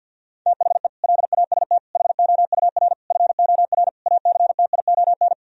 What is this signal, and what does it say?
Morse code, heard as a keyed tone, about 51 wpm, 700 Hz. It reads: THE LAST HOUR FOR ACTION
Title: Sentences from Top 1000 Words - Review Part 4 - 50wpm